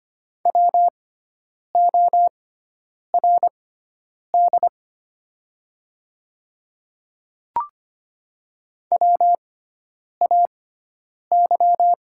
Morse code, heard as a keyed tone, about 25 wpm, spaced as for 9 wpm, 700 Hz. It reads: WORD WAY
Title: 6x Character Spacing - Top 200 Words - 25wpm